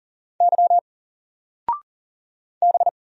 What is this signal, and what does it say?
Morse code, heard as a keyed tone, about 40 wpm, 700 Hz. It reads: Y B
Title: Single Letter-Number Alt 40wpm